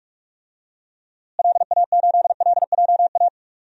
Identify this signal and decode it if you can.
Morse code, heard as a keyed tone, about 44 wpm, 700 Hz. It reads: LA8LJA
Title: DXCC-100 Call Signs - 6 Characters - Repeated 2x - 45wpm